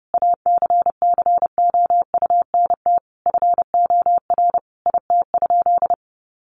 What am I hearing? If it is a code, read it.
Morse code, heard as a keyed tone, about 30 wpm, 700 Hz. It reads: ACCOUNT FOR IT?
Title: Sentences from Top 900 Words 30wpm